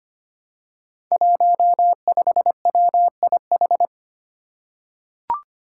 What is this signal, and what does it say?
Morse code, heard as a keyed tone, about 25 wpm, 700 Hz. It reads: 15WIH
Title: Five Challenging Letter-Number Combinations 25wpm